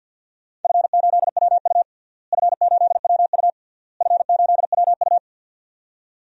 Morse code, heard as a keyed tone, about 53 wpm, 700 Hz. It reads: F8WU F8WU F8WU
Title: DXCC-100 Call Signs - 4 Characters - Repeated 3x - 50wpm